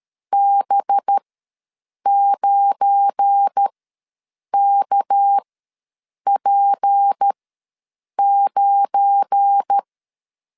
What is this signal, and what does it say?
Morse code, heard as a keyed tone, about 13 wpm, spaced as for 7 wpm, 800 Hz. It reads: B9KP9
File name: LEZIONI / 7 GF9Z / 7 GF9 ws 12wpm 3344